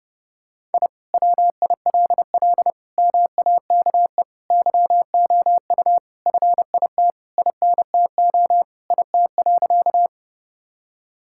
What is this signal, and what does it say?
Morse code, heard as a keyed tone, about 32 wpm, 700 Hz. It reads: I WILL MAKE YOU FIT INTO IT.
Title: Sentences from Top 1000 Words - Review Part 5 - 30wpm